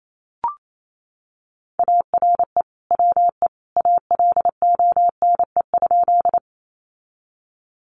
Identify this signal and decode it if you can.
Morse code, heard as a keyed tone, about 28 wpm, 700 Hz. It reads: ARE WE ALONE?
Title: Sentences from Top 1000 Words 28wpm